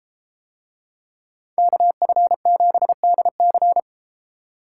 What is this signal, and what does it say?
Morse code, heard as a keyed tone, about 31 wpm, 700 Hz. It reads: KF7DC